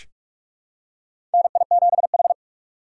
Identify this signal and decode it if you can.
Morse code, heard as a keyed tone, about 42 wpm, 700 Hz. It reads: NI7H
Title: USA Call Signs - 4 Characters - 6x Character Spacing - 45wpm